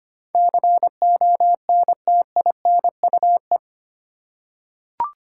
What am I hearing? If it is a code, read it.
Morse code, heard as a keyed tone, about 25 wpm, 700 Hz. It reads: CONTINUE
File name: 8 Letter Words - First 3 Letters Given - 25wpm